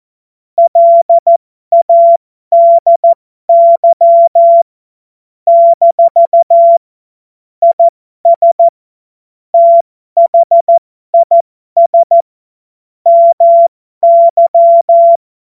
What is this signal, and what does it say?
Morse code, heard as a keyed tone, about 14 wpm, spaced as for 12 wpm, 680 Hz. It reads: LADY - IS THIS MY
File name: Morsekurs / Witze